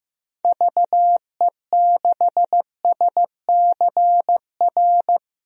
Morse code, heard as a keyed tone, about 15 wpm, 700 Hz. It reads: VE6SCR
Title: DXCC-100 Call Signs - 6 Characters - Repeated 3x - 15wpm